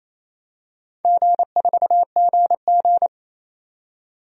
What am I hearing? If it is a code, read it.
Morse code, heard as a keyed tone, about 28 wpm, 700 Hz. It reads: G4GG
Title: DXCC-100 Call Signs - 4 Characters - 28wpm